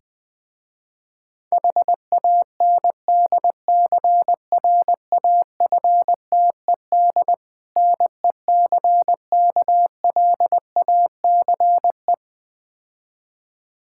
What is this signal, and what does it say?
Morse code, heard as a keyed tone, about 21 wpm, 700 Hz. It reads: HANDCRAFTED NECKLACE
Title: Difficult Words with Hints - 20wpm